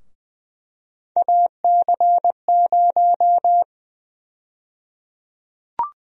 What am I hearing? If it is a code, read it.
Morse code, heard as a keyed tone, about 21 wpm, 700 Hz. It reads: AC0
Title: USA Call Sign Prefixes - Repeated 2x - 20wpm